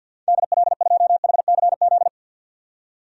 Morse code, heard as a keyed tone, about 52 wpm, 700 Hz. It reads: DL2HCZ
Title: DXCC-100 Call Signs - 6 Characters - 50wpm